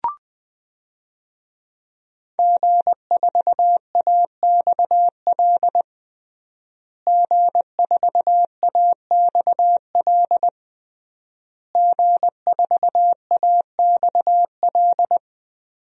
Morse code, heard as a keyed tone, about 21 wpm, 700 Hz. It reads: G4AXL G4AXL G4AXL
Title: DXCC-100 Call Signs - 5 Characters - Repeated 3x - 20wpm